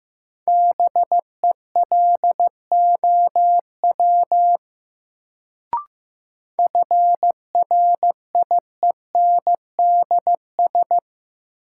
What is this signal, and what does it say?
Morse code, heard as a keyed tone, about 15 wpm, 700 Hz. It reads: BELOW FRIENDS
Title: Sentences from Top 1000 Words - Review Part 5 - New Word Drill - 15wpm